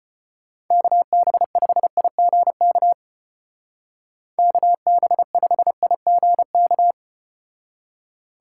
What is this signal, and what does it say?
Morse code, heard as a keyed tone, about 32 wpm, 700 Hz. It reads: KB5IGK KB5IGK